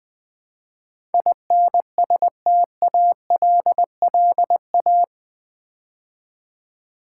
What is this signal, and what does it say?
Morse code, heard as a keyed tone, about 20 wpm, 700 Hz. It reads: INSTALLA
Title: Build A Word - 4 Letters At A Time - 12-Letter Words - 20wpm